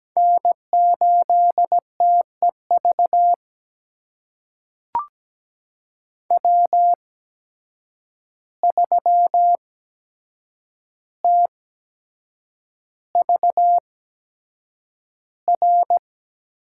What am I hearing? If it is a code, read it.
Morse code, heard as a keyed tone, about 17 wpm, spaced as for 7 wpm, 700 Hz. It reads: N8TEV W3TVR
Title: USA Call Signs - 5 Characters - 8x Character Spacing - 17wpm